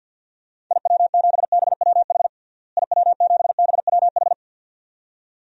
Morse code, heard as a keyed tone, about 48 wpm, 700 Hz. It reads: IW7BWH IW7BWH
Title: DXCC-100 Call Signs - 6 Characters - Repeated 2x - 50wpm